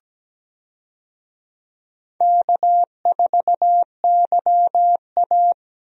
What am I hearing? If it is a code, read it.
Morse code, heard as a keyed tone, about 17 wpm, 700 Hz. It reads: K4YA
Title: USA Call Signs - 4 Characters - Repeated 2x - 17wpm